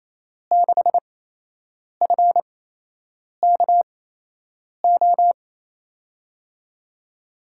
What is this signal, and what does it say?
Morse code, heard as a keyed tone, about 28 wpm, spaced as for 8 wpm, 700 Hz. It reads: 6FKO
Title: USA Call Signs - 6 Characters - 8x Character Spacing - 28wpm